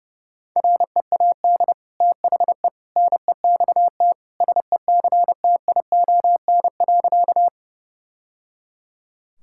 Morse code, heard as a keyed tone, about 32 wpm, 700 Hz. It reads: READ THE NEXT SECTION.
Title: Sentences from Top 1000 Words - Review Part 5 - 30wpm